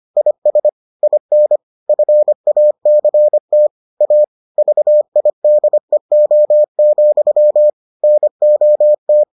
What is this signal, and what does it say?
Morse code, heard as a keyed tone, about 25 wpm, 600 Hz. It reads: IS IN FACT A VIDEO, NOT